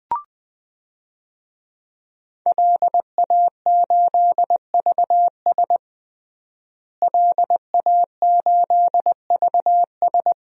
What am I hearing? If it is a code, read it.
Morse code, heard as a keyed tone, about 20 wpm, 700 Hz. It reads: LA8VS LA8VS
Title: DXCC-100 Call Signs - 5 Characters - Repeated 3x - 20wpm